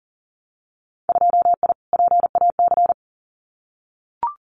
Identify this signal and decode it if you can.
Morse code, heard as a keyed tone, about 42 wpm, 700 Hz. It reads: 2I PAC
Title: Winter Field Day Exchanges Encore 40wpm